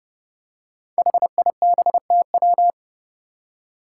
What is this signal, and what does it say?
Morse code, heard as a keyed tone, about 30 wpm, 700 Hz. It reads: HIBTW
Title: Five Challenging Letter-Number Combinations 30wpm